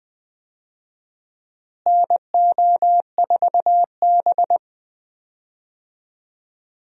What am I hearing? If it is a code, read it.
Morse code, heard as a keyed tone, about 20 wpm, 700 Hz. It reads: NO4B
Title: USA Call Signs - 4 Characters - Repeated 2x - 20wpm